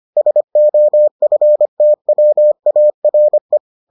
Morse code, heard as a keyed tone, about 25 wpm, 600 Hz. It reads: SOFTWARE